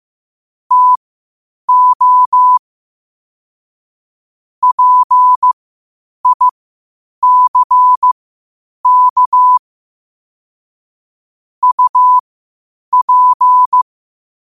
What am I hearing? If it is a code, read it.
Morse code, heard as a keyed tone, about 15 wpm, spaced as for 8 wpm, 1000 Hz. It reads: TO PICK UP
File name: qotdfiles / 8wpm